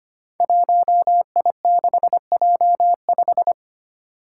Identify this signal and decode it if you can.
Morse code, heard as a keyed tone, about 25 wpm, 700 Hz. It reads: 1I6J5
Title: Five Challenging Letter-Number Combinations 25wpm